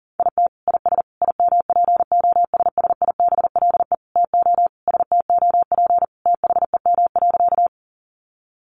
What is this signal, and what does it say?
Morse code, heard as a keyed tone, about 40 wpm, 700 Hz. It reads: IT IS IMPOSSIBLE TO STOP THEM.
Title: Sentences from Top 1100 Words 40wpm